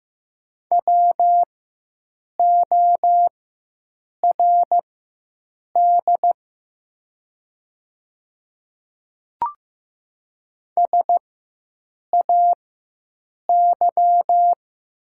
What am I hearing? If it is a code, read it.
Morse code, heard as a keyed tone, about 15 wpm, spaced as for 7 wpm, 700 Hz. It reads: WORD SAY
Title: 4x Character Spacing - Top 300 Words - 15wpm